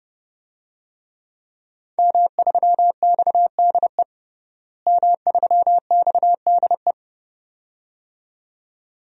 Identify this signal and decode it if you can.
Morse code, heard as a keyed tone, about 30 wpm, 700 Hz. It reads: M3XDE M3XDE
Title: DXCC-100 Call Signs - 5 Characters - Repeated 2x - 30wpm